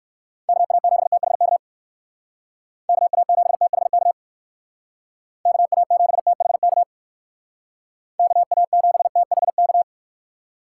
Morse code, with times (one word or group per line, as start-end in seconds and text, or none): 0.5-1.6 KA7THK
2.9-4.1 KA7THK
5.5-6.8 KA7THK
8.2-9.8 KA7THK